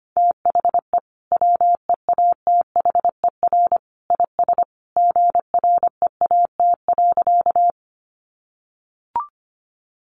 Morse code, heard as a keyed tone, about 25 wpm, 700 Hz. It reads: THE WEATHER IS GREAT.
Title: Sentences from Top 800 Words 25wpm